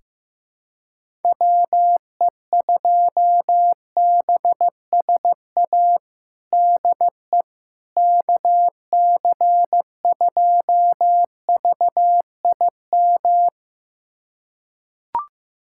Morse code, heard as a keyed tone, about 15 wpm, 700 Hz. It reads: WE2BSA DE KC2VIM
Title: USA Callsign DE Callsign - 6 Characters - 15wpm